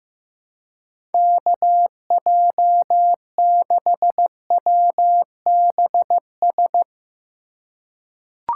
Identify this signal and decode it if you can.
Morse code, heard as a keyed tone, about 15 wpm, 700 Hz. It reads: KJ6WBS
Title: USA Call Signs - 6 Characters - 4x Character Spacing - 15wpm